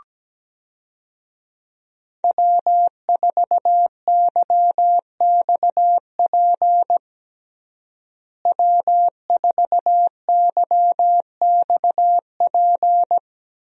Morse code, heard as a keyed tone, about 17 wpm, 700 Hz. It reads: W4YXP W4YXP
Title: USA Call Signs - 5 Characters - Repeated 3x - 17wpm